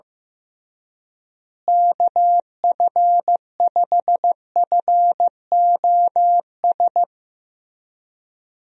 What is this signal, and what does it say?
Morse code, heard as a keyed tone, about 15 wpm, 700 Hz. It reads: KF5FOS
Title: USA Call Signs - 6 Characters - Repeated 3x - 15wpm